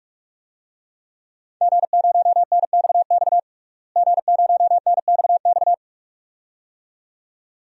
Morse code, heard as a keyed tone, about 46 wpm, 700 Hz. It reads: G0NXX G0NXX
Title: DXCC-100 Call Signs - 5 Characters - Repeated 2x - 45wpm